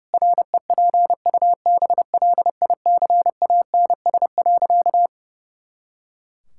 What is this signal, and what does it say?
Morse code, heard as a keyed tone, about 32 wpm, 700 Hz. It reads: REPUBLICANS.
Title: Pangram Challenge v2 - 30wpm